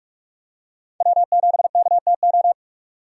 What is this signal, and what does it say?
Morse code, heard as a keyed tone, about 44 wpm, 700 Hz. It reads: W7KTO
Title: USA Call Signs - 5 Characters - 45wpm